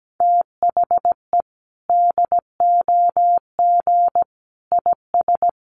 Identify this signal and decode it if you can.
Morse code, heard as a keyed tone, about 17 wpm, 700 Hz. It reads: THE DOG IS
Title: Sentences from Top 400 Words 17wpm